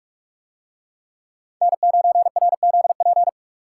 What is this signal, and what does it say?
Morse code, heard as a keyed tone, about 44 wpm, 700 Hz. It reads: N9RZP